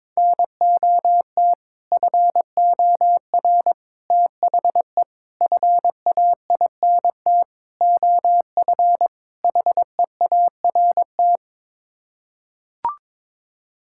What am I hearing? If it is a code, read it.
Morse code, heard as a keyed tone, about 21 wpm, 700 Hz. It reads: NOT FOR THE FAINT OF HEART
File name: practice / 22wpm / Sets of 6 Words - 22wpm